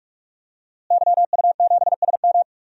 Morse code, heard as a keyed tone, about 46 wpm, 700 Hz. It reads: YU7SM
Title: DXCC-100 Call Signs - 5 Characters - Repeated 3x - 45wpm